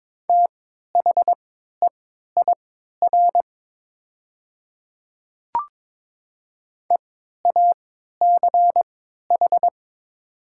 Morse code, heard as a keyed tone, about 22 wpm, spaced as for 13 wpm, 700 Hz. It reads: THEIR EACH
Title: 3x Character Spacing - Top 100 Words - 22wpm